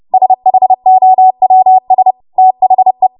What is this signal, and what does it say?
Morse code, heard as a keyed tone, about 30 wpm, 750 Hz. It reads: SHOWS THE
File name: Morse / 30WPM